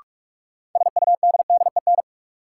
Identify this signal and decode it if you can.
Morse code, heard as a keyed tone, about 46 wpm, 700 Hz. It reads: SUDDEN
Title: Sets of 4 Words - New Word Drill - 45wpm